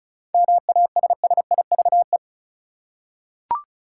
Morse code, heard as a keyed tone, about 34 wpm, 700 Hz. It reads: MASSIVE
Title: Seven Letter Words - 35wpm